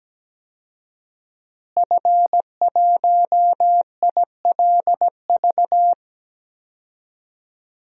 Morse code, heard as a keyed tone, about 17 wpm, 700 Hz. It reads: F1ILV
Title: DXCC-100 Call Signs - 5 Characters - Repeated 3x - 17wpm